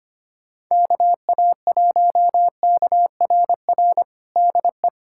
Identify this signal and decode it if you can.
Morse code, heard as a keyed tone, about 25 wpm, 700 Hz. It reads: KA1KRR DE
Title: USA Callsign DE Callsign - 6 Characters - 25wpm